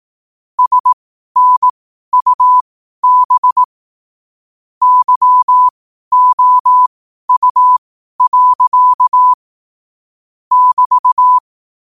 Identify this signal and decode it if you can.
Morse code, heard as a keyed tone, about 18 wpm, spaced as for 12 wpm, 1000 Hz. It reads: SNUB YOU. <BT>